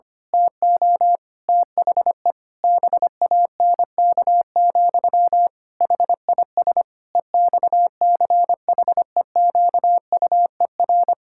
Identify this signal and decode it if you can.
Morse code, heard as a keyed tone, about 25 wpm, 700 Hz. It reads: TO THE BANK, HIS EXCHEQUER
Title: Pangram Challenge v2 - 25wpm